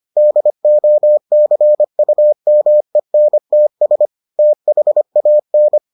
Morse code, heard as a keyed tone, about 25 wpm, 600 Hz. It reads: DOCUMENTS THAN